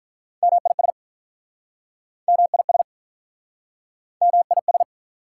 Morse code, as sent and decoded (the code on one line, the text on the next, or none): -- .. ... / -- .. ... / -- .. ...
MIS MIS MIS